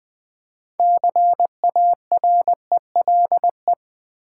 Morse code, heard as a keyed tone, about 20 wpm, 700 Hz. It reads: CARELE
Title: Build A Word - 3 Letters At A Time - 8-Letter Words - 20wpm